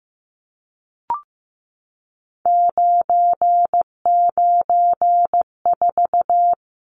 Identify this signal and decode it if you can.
Morse code, heard as a keyed tone, about 15 wpm, 700 Hz. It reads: 994
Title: Numbers - 3-Digit 15wpm